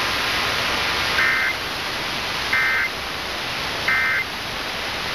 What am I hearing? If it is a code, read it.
Emergency Alert System (SAME) tones, heard as harsh buzzing bursts. It is heard from 1.1 to 4.3 s.